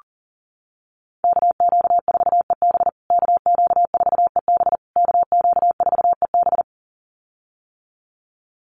Morse code, heard as a keyed tone, about 40 wpm, 700 Hz. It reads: KQ4EB KQ4EB KQ4EB
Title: USA Call Signs Repeated 3x 40wpm